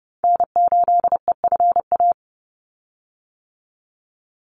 Morse code, heard as a keyed tone, about 30 wpm, 700 Hz. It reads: N8EFA
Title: USA Call Signs 30wpm